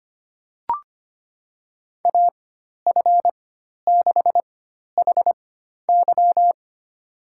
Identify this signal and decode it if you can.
Morse code, heard as a keyed tone, about 25 wpm, spaced as for 12 wpm, 700 Hz. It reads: AF6HY
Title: USA Call Signs - 5 Characters - 4x Character Spacing - 25wpm